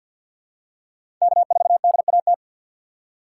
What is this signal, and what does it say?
Morse code, heard as a keyed tone, about 53 wpm, 700 Hz. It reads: K4DAT
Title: USA Call Signs - 3x Character Spacing - 50wpm